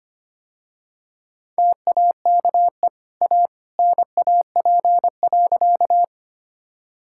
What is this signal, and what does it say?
Morse code, heard as a keyed tone, about 25 wpm, 700 Hz. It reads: TAKE A NAP.
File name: Easy Sentences with Hints - 25wpm